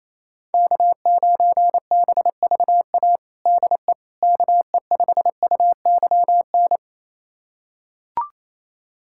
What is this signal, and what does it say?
Morse code, heard as a keyed tone, about 28 wpm, 700 Hz. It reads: K9BVA DE KE5UYN
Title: USA Callsign DE Callsign 28wpm